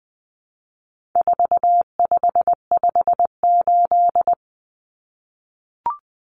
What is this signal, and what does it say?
Morse code, heard as a keyed tone, about 20 wpm, 700 Hz. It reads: 4558
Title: Numbers - 4-Digit 20wpm